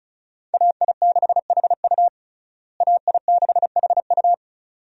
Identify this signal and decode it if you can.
Morse code, heard as a keyed tone, about 34 wpm, 700 Hz. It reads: AI6HU AI6HU
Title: USA Call Signs - 5 Characters - Repeated 2x - 35wpm